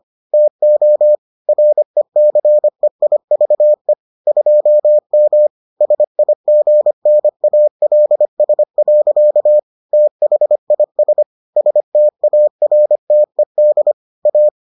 Morse code, heard as a keyed tone, about 25 wpm, 600 Hz. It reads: TO RECEIVE 2M SIGNALS. THIS STARTED A